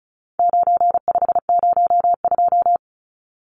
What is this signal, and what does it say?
Morse code, heard as a keyed tone, about 34 wpm, 700 Hz. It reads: 9502
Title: Numbers - 4-Digit 35wpm